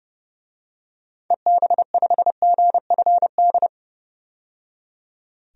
Morse code, heard as a keyed tone, about 30 wpm, 700 Hz. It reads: EB5GFD
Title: DXCC-100 Call Signs - 6 Characters - 30wpm